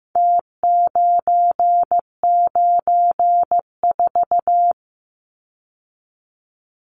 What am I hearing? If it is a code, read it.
Morse code, heard as a keyed tone, about 15 wpm, 700 Hz. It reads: T994
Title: Numbers - Contest Style - 4-Digit 15wpm